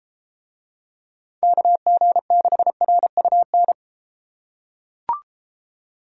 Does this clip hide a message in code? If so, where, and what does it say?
Yes: Morse code from 1.4 to 3.7 s, heard as a keyed tone, about 31 wpm, 700 Hz. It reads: KG6RUN